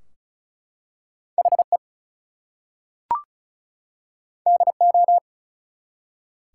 Morse code, heard as a keyed tone, about 33 wpm, 700 Hz. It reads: HE DO
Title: Two Letter Words - 35wpm